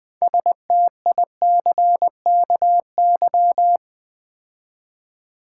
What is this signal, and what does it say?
Morse code, heard as a keyed tone, about 20 wpm, 700 Hz. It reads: STICKY
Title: Build A Word - 2 Letters At A Time - 6-Letter Words - 20wpm